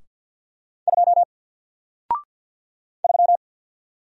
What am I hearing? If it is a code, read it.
Morse code, heard as a keyed tone, about 48 wpm, 700 Hz. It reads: 2 3
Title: Single Letter-Number Alt 50wpm